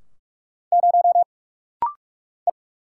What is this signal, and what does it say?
Morse code, heard as a keyed tone, about 44 wpm, 700 Hz. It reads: 0 E
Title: Single Letter-Number - Rapid-Fire 45wpm